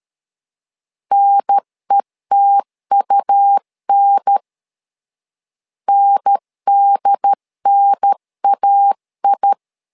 Morse code, heard as a keyed tone, about 13 wpm, 800 Hz. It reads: NETUN NDNAI
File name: LEZIONI / 2 UHDN / 2 UHDN 12wpm 1456